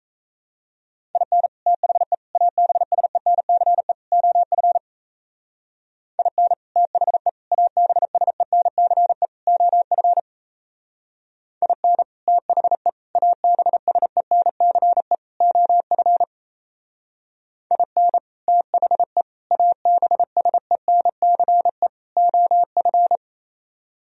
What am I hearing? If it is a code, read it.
Morse code, heard as a keyed tone, about 36 wpm, 700 Hz. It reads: IN THE ABSENCE OF IN THE ABSENCE OF IN THE ABSENCE OF IN THE ABSENCE OF